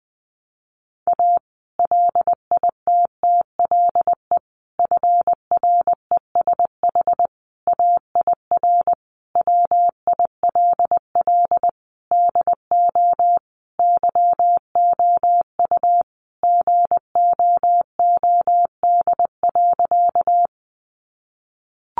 Morse code, heard as a keyed tone, about 20 wpm, 700 Hz. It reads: A LITTLE FRESH AIR WILL DO YOU GOOD.
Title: Sentences from Top 1200 Words 20wpm